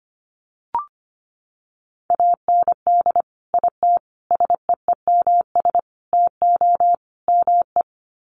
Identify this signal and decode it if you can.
Morse code, heard as a keyed tone, about 25 wpm, 700 Hz. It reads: AND IT SEEMS TO ME
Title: Sets of 5 Words Encore2 25wpm